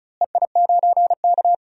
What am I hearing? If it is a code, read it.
Morse code, heard as a keyed tone, about 34 wpm, 700 Hz. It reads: EI9K
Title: DXCC-100 Call Signs - 4 Characters - Repeated 2x - 35wpm